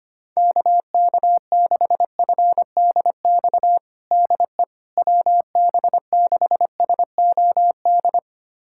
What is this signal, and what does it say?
Morse code, heard as a keyed tone, about 25 wpm, 700 Hz. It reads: KK6FDX DE WB6SOD
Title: USA Callsign DE Callsign - 6 Characters - 25wpm